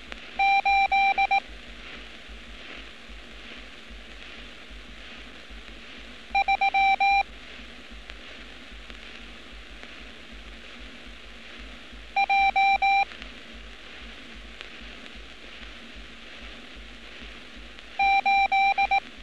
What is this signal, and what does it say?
Morse code, heard as a keyed tone, about 19 wpm, 790 Hz. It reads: 8 3 J 8
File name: training / NAVPERS 11200-11211 / NAVPERS 11206 RA RB